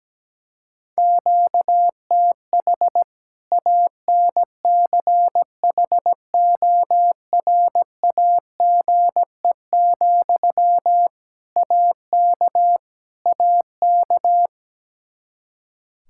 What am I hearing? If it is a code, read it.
Morse code, heard as a keyed tone, about 17 wpm, 700 Hz. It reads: QTH ANCHORAGE, AK AK
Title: Practice QSO - Exchange 1 - Beginning - 17wpm